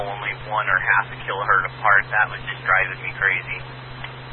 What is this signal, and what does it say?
Emergency Alert System (SAME) tones, heard as harsh buzzing bursts, with speech. It is heard from 0.6 to 3.5 s.